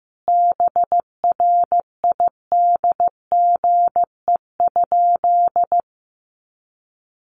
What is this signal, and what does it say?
Morse code, heard as a keyed tone, about 15 wpm, 700 Hz. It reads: BRIDGE?
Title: Sentences from Top 1200 Words 15wpm